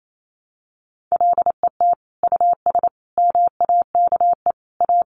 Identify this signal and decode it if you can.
Morse code, heard as a keyed tone, about 28 wpm, 700 Hz. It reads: LET US MAKE A
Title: Sentences from Top 1200 Words 28wpm